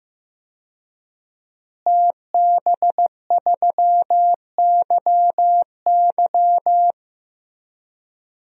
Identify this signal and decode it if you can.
Morse code, heard as a keyed tone, about 16 wpm, 700 Hz. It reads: TB3YY
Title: DXCC-100 Call Signs - 5 Characters - Repeated 2x - 15wpm